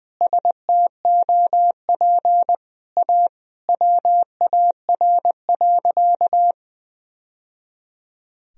Morse code, heard as a keyed tone, about 21 wpm, 700 Hz. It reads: STOP A WAR.
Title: Sentences from Top 1000 Words - Review Part 2 - 20wpm